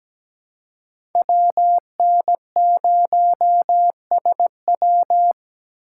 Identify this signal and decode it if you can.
Morse code, heard as a keyed tone, about 17 wpm, 700 Hz. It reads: WN0SW
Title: USA Call Signs - 5 Characters - 3x Character Spacing - 17wpm